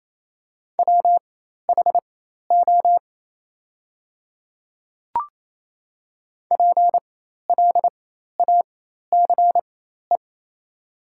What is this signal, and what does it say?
Morse code, heard as a keyed tone, about 28 wpm, spaced as for 13 wpm, 690 Hz. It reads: WHO PLACE
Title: 4x Character Spacing - Top 300 Words - 28wpm